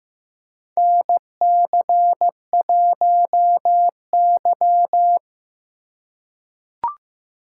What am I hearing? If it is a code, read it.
Morse code, heard as a keyed tone, about 15 wpm, 700 Hz. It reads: NC1Y
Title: USA Call Signs - 4 Characters - 3x Character Spacing - 15wpm